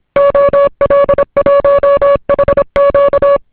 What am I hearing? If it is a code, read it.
Morse code, heard as a keyed tone, about 26 wpm, 560 Hz. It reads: OL1HQ